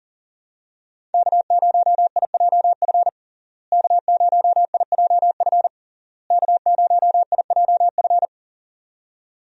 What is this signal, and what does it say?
Morse code, heard as a keyed tone, about 37 wpm, 700 Hz. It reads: K0IJF K0IJF K0IJF
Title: USA Call Signs - 5 Characters - Repeated 3x - 40wpm